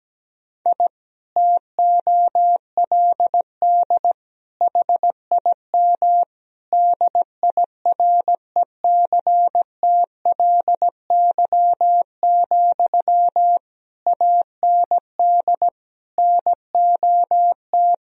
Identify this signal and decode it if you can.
Morse code, heard as a keyed tone, about 17 wpm, 700 Hz. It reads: I TOLD HIM DIRECTLY, AND NOT
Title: Sentences from Top 1000 Words - Review Part 1 - 17wpm